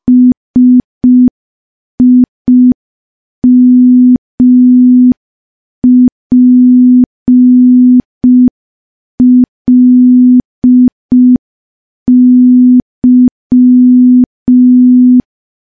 Morse code, heard as a keyed tone, about 5 wpm, 250 Hz. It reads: SIMPLY
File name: morse / morse-training-05WPM